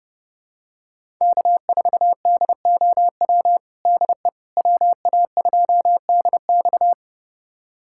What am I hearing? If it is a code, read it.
Morse code, heard as a keyed tone, about 30 wpm, 700 Hz. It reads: K4DOW DE WA2DX
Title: USA Callsign DE Callsign - 5 Characters - 30wpm